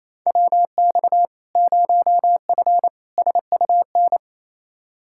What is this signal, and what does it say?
Morse code, heard as a keyed tone, about 28 wpm, 700 Hz. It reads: WX 0F SUN
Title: Practice QSO - Exchange 2 - Info - 28wpm